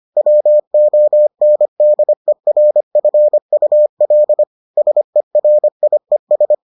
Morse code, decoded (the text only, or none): WONDERFUL SERIES